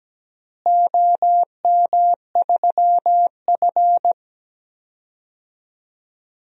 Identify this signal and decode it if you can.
Morse code, heard as a keyed tone, about 17 wpm, 700 Hz. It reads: OM3F
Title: DXCC-100 Call Signs - 4 Characters - Repeated 2x - 17wpm